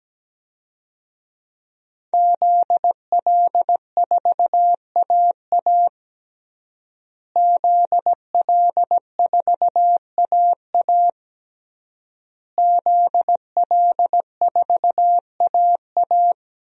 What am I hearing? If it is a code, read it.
Morse code, heard as a keyed tone, about 17 wpm, 700 Hz. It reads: ZL4AA ZL4AA ZL4AA
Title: DXCC-100 Call Signs - 5 Characters - Repeated 3x - 17wpm